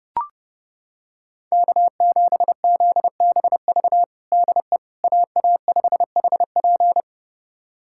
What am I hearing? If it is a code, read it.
Morse code, heard as a keyed tone, about 30 wpm, 700 Hz. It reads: K7ZBV DE AA5HP
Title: USA Callsign DE Callsign - 5 Characters - 30wpm